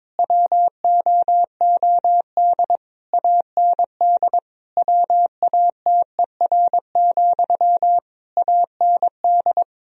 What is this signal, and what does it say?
Morse code, heard as a keyed tone, about 22 wpm, 700 Hz. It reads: WOOD AND WATER, AND
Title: Sentences from Top 1000 Words - Review Part 2 - 22wpm